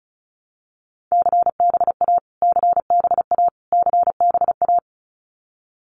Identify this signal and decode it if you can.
Morse code, heard as a keyed tone, about 34 wpm, 700 Hz. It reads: CBA CBA CBA
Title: Abbreviations Repeated 3x - 35wpm